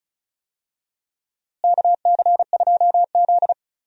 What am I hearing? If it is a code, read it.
Morse code, heard as a keyed tone, about 36 wpm, 700 Hz. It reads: KC2Z
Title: USA Call Signs - 4 Characters - Repeated 2x - 35wpm